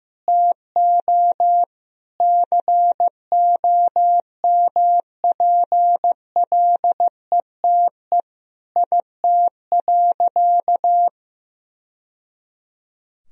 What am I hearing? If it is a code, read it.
Morse code, heard as a keyed tone, about 15 wpm, 700 Hz. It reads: TO COMPLETE IT.
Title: Sentences from Top 1000 Words - Review Part 3 - 15wpm